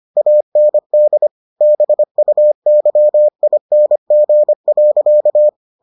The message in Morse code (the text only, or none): AND BUYING.